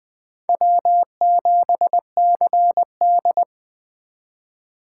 Morse code, heard as a keyed tone, about 24 wpm, 700 Hz. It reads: W7CD
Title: Speed Racing - USA 1x2 Call Signs 20wpm